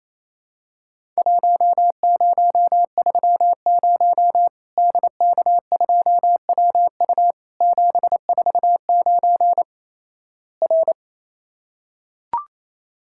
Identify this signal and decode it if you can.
Morse code, heard as a keyed tone, about 28 wpm, 700 Hz. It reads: 1030 DK2WU 749 R
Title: Worked All Europe DX Contest - QTC Exchange - 28wpm